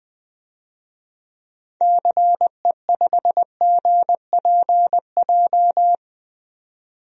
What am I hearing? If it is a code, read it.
Morse code, heard as a keyed tone, about 20 wpm, 700 Hz. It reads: CE5GPJ
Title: DXCC-100 Call Signs - 6 Characters - 20wpm